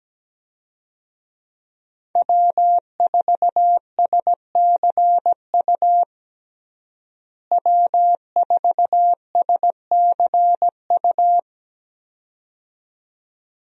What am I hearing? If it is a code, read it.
Morse code, heard as a keyed tone, about 17 wpm, 700 Hz. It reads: W4SCU W4SCU
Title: USA Call Signs - 5 Characters - Repeated 2x - 17wpm